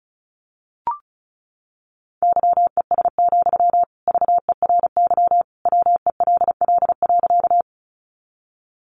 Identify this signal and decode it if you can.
Morse code, heard as a keyed tone, about 34 wpm, 700 Hz. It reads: YES, VERY WELL.
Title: Sentences from Top 500 Words 35wpm